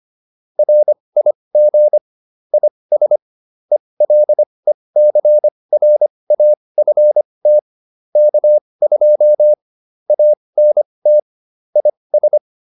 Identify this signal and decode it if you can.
Morse code, heard as a keyed tone, about 25 wpm, spaced as for 20 wpm, 600 Hz. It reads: RIG IS ELECRAFT K2 ANT IS